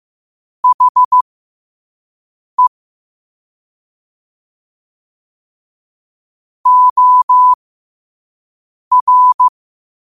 Morse code, heard as a keyed tone, about 15 wpm, spaced as for 5 wpm, 1000 Hz. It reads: HE OR
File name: qotdfiles / 5wpm